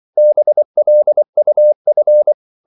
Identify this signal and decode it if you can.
Morse code, heard as a keyed tone, about 24 wpm, 600 Hz. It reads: BLUF
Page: Korte danske ord 24wpm